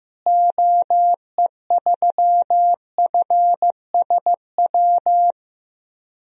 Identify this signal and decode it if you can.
Morse code, heard as a keyed tone, about 15 wpm, 700 Hz. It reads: OE3FSW
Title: DXCC-100 Call Signs - 6 Characters - Repeated 3x - 15wpm